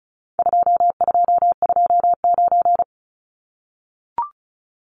Morse code, heard as a keyed tone, about 34 wpm, 700 Hz. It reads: 2229
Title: Numbers - 4-Digit 35wpm